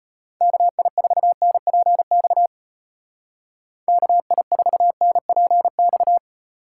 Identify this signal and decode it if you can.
Morse code, heard as a keyed tone, about 32 wpm, 700 Hz. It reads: KI4NPX KI4NPX